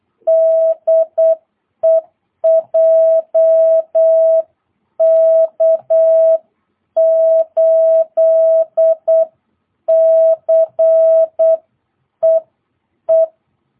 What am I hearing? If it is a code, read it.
Morse code, heard as a keyed tone, about 8 wpm, 650 Hz. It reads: DE JK8CEE
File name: jh8cbh / 2022cw